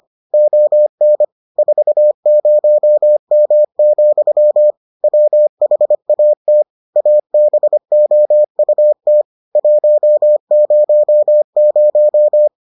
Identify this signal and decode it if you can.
Morse code, heard as a keyed tone, about 25 wpm, 600 Hz. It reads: ON 40M, WHAT ABOUT 100